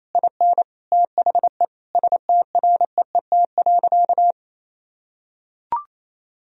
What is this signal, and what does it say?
Morse code, heard as a keyed tone, about 28 wpm, 700 Hz. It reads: IN THE STREET.
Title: Sentences from Top 1000 Words - Review Part 3 - 28wpm